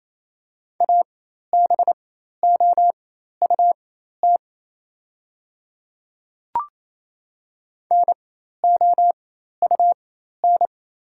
Morse code, heard as a keyed tone, about 28 wpm, spaced as for 13 wpm, 700 Hz. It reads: ABOUT NOUN
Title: 4x Character Spacing - Top 500 Words - 28wpm